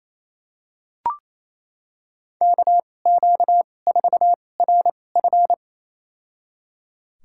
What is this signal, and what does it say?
Morse code, heard as a keyed tone, about 28 wpm, spaced as for 20 wpm, 700 Hz. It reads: KQ4RF
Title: USA Call Signs - 5 Characters - 2x Character Spacing - 28wpm